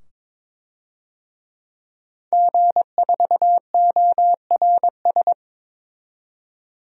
Morse code, heard as a keyed tone, about 21 wpm, 700 Hz. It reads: G4ORS
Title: DXCC-100 Call Signs - 5 Characters - 22wpm